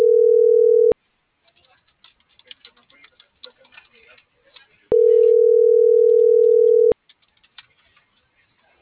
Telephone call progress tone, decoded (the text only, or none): ringback tone